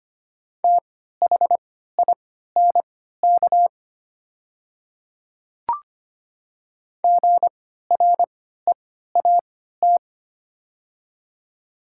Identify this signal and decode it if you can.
Morse code, heard as a keyed tone, about 25 wpm, spaced as for 14 wpm, 700 Hz. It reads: THINK GREAT
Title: 3x Character Spacing - Top 200 Words - 25wpm